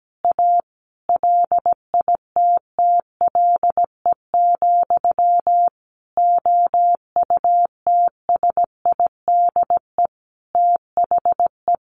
Morse code, heard as a keyed tone, about 17 wpm, 700 Hz. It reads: A LITTLE, OUTSIDE THE
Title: Sentences from Top 900 Words 17wpm